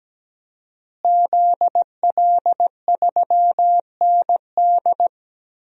Morse code, heard as a keyed tone, about 17 wpm, 700 Hz. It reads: ZL3ND
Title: DXCC-100 Call Signs - 5 Characters - Repeated 3x - 17wpm